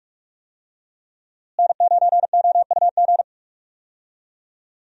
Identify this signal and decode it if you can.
Morse code, heard as a keyed tone, about 47 wpm, 700 Hz. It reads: N9OUG
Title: USA Call Signs - 5 Characters - Repeated 2x - 45wpm